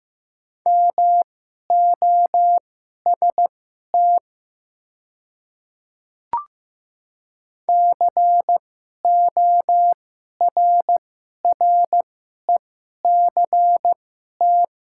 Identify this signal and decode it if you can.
Morse code, heard as a keyed tone, about 15 wpm, spaced as for 11 wpm, 700 Hz. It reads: MOST CORRECT
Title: 2x Character Spacing - Top 500 Words - 15wpm